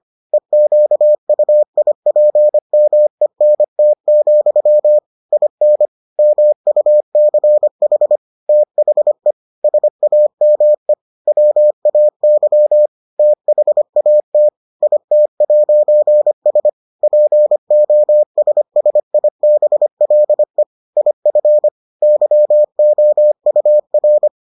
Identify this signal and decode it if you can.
Morse code, heard as a keyed tone, about 25 wpm, 600 Hz. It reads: EQUIPMENT, IN MUCH THE SAME WAY THAT IT'S POSSIBLE IF YOUR